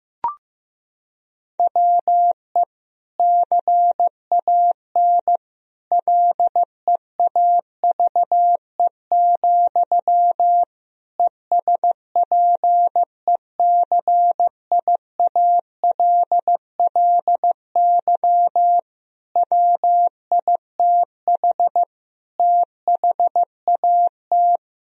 Morse code, heard as a keyed tone, about 15 wpm, 700 Hz. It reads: WE CAN LEAVE, ESPECIALLY WITH THAT
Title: Sentences from Top 1000 Words - Review Part 4 - 15wpm